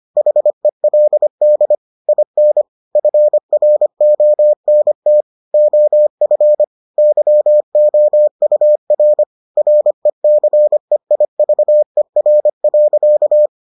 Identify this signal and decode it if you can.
Morse code, heard as a keyed tone, about 25 wpm, 600 Hz. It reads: HELD IN FRONT OF YOUR RECEIVER.